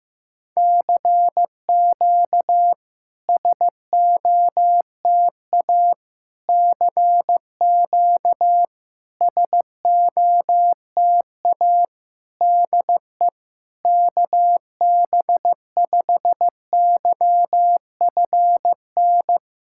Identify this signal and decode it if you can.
Morse code, heard as a keyed tone, about 15 wpm, 700 Hz. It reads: CQ SOTA CQ SOTA DE KB5YFN
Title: SOTA QSO Practice 15wpm